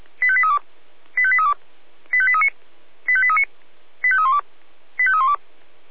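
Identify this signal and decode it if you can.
Five-tone selective call (ZVEI), heard as quick runs of five tones, tones of about 80 ms. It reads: 85832 85832 85839 85839 85312 85312
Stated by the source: ZVEI five-tone selective call received on an SDR.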